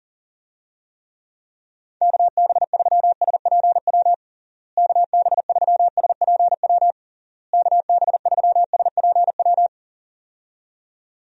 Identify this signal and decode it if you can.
Morse code, heard as a keyed tone, about 40 wpm, 700 Hz. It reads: KB3SPW KB3SPW KB3SPW
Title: USA Call Signs - 6 Characters - Repeated 3x - 40wpm